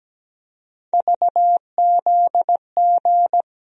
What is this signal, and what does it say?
Morse code, heard as a keyed tone, about 17 wpm, 700 Hz. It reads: VZG
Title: USA Call Sign Postfixes - 3 Letters 17wpm